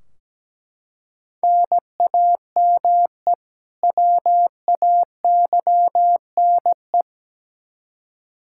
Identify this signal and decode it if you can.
Morse code, heard as a keyed tone, about 17 wpm, 700 Hz. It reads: NAME WAYNE
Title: Practice QSO - Exchange 1 - Beginning - 17wpm